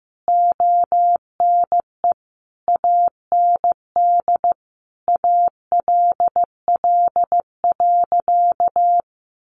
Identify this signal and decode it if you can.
Morse code, heard as a keyed tone, about 15 wpm, 700 Hz. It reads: ONE AND ALL.
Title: Sentences from Top 200 Words 15wpm